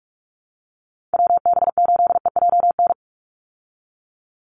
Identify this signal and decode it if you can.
Morse code, heard as a keyed tone, about 44 wpm, 700 Hz. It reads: WB8EJN
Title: Call Signs Repeated 2x - 45wpm